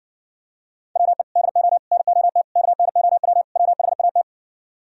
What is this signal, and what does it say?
Morse code, heard as a keyed tone, about 52 wpm, 700 Hz. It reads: WE DO NOT KNOW WHAT